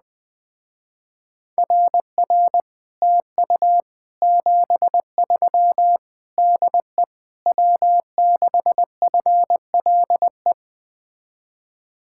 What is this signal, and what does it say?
Morse code, heard as a keyed tone, about 20 wpm, 700 Hz. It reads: RR TU 73 DE W6FLE
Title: Speed-Racing Common QSO Phrases Encore - 17wpm